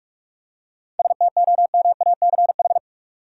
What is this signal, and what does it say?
Morse code, heard as a keyed tone, about 46 wpm, 700 Hz. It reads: STOMACH
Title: Seven Letter Words - 45wpm